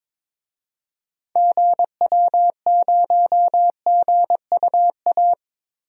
Morse code, heard as a keyed tone, about 22 wpm, 700 Hz. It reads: GW0GUA
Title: DXCC-100 Call Signs - 6 Characters - Repeated 2x - 22wpm